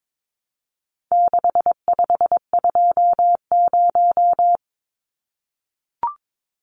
Morse code, heard as a keyed tone, about 22 wpm, 700 Hz. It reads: 6520
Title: Numbers - 4-Digit 22wpm